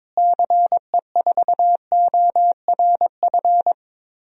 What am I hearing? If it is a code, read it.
Morse code, heard as a keyed tone, about 21 wpm, 700 Hz. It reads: CE4ORF
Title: DXCC-100 Call Signs - 6 Characters - Repeated 3x - 22wpm